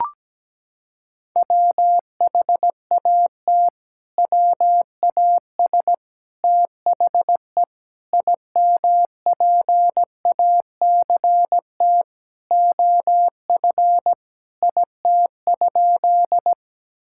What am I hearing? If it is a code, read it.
Morse code, heard as a keyed tone, about 17 wpm, 700 Hz. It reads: WHAT WAS THE IMPACT OF IT?
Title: Sentences from Top 1000 Words - Review Part 6 - 17wpm